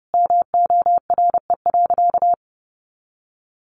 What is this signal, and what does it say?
Morse code, heard as a keyed tone, about 30 wpm, 700 Hz. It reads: MORE.
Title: Sentences from Top 1100 Words 30wpm